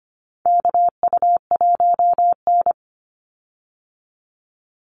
Morse code, heard as a keyed tone, about 25 wpm, 700 Hz. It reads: KU1N
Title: Call Signs Repeated 2x - 25wpm